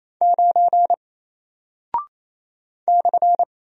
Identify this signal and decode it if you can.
Morse code, heard as a keyed tone, about 27 wpm, 700 Hz. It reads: 9 /
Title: Single Character (Letter:Number:Punctuation:Prosign) Alt 28wpm